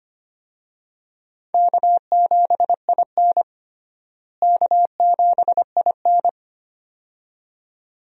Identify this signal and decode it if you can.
Morse code, heard as a keyed tone, about 25 wpm, 700 Hz. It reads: K7IN K7IN
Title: USA Call Signs - 4 Characters - Repeated 2x - 25wpm